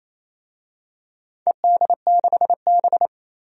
Morse code, heard as a keyed tone, about 28 wpm, 700 Hz. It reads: ED6B
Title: DXCC-100 Call Signs - 4 Characters - 28wpm